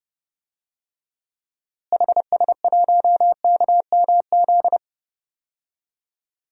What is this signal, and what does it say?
Morse code, heard as a keyed tone, about 30 wpm, 700 Hz. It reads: HS1KMZ
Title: DXCC-100 Call Signs - 6 Characters - Repeated 2x - 30wpm